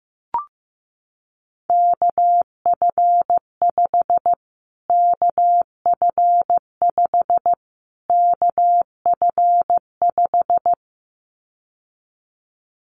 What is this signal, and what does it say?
Morse code, heard as a keyed tone, about 15 wpm, 700 Hz. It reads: KF5 KF5 KF5
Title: Callsign Prefixes for Top 25 DXCC - Repeated 3x - 15wpm